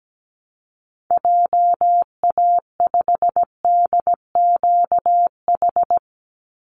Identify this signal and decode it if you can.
Morse code, heard as a keyed tone, about 17 wpm, 700 Hz. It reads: JA5DQH
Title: Call Signs Repeated 3x - 17wpm